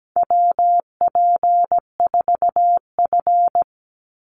Morse code, heard as a keyed tone, about 17 wpm, 700 Hz. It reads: WP4F
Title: Call Signs Repeated 2x - 17wpm